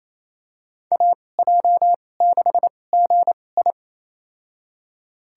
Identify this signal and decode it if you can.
Morse code, heard as a keyed tone, about 28 wpm, spaced as for 20 wpm, 700 Hz. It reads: AJ6GI
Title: USA Call Signs - 5 Characters - 2x Character Spacing - 28wpm